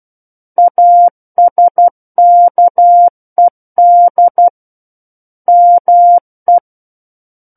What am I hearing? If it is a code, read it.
Morse code, heard as a keyed tone, about 12 wpm, 700 Hz. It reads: ASKED ME
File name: Book One / 03-Chapter 3 - 12 WPM